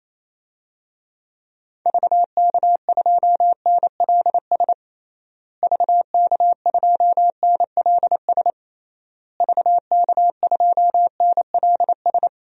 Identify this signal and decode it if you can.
Morse code, heard as a keyed tone, about 28 wpm, 700 Hz. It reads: VK2NLS VK2NLS VK2NLS
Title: DXCC-100 Call Signs - 6 Characters - Repeated 3x - 28wpm